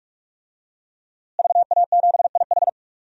Morse code, heard as a keyed tone, about 46 wpm, 700 Hz. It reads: VA7IH
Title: DXCC-100 Call Signs - 5 Characters - Repeated 2x - 45wpm